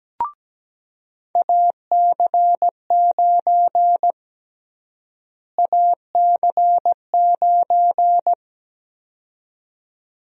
Morse code, heard as a keyed tone, about 17 wpm, 700 Hz. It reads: AC9 AC9
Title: USA Call Sign Prefixes - Repeated 2x - 17wpm